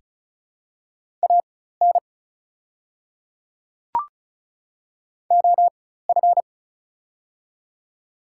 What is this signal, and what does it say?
Morse code, heard as a keyed tone, about 36 wpm, spaced as for 16 wpm, 700 Hz. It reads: AN OF
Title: 4x Character Spacing - Top 100 Words - 35wpm